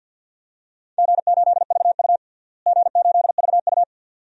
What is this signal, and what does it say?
Morse code, heard as a keyed tone, about 52 wpm, 700 Hz. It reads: G8VU G8VU
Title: DXCC-100 Call Signs - 4 Characters - Repeated 2x - 50wpm